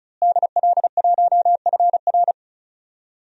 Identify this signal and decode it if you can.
Morse code, heard as a keyed tone, about 34 wpm, 700 Hz. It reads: DL1FR
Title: DXCC-100 Call Signs - 5 Characters - Repeated 2x - 35wpm